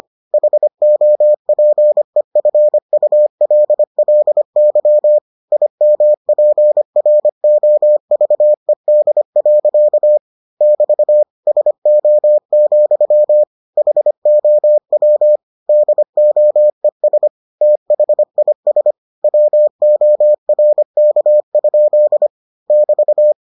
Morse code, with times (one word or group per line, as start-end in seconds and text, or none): 0.3-5.2 HOPEFULLY
5.5-10.2 IMPROVED.
10.6-11.2 <BT>
11.5-13.4 SO,
13.8-15.4 HOW
15.7-17.3 DOES
17.6-18.9 THIS
19.2-22.3 WORK?
22.7-23.3 <BT>